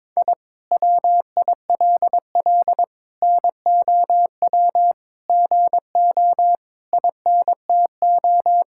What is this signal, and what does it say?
Morse code, heard as a keyed tone, about 22 wpm, 700 Hz. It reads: I WILL NOW GO INTO
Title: Sentences from Top 1000 Words - Review Part 6 - 22wpm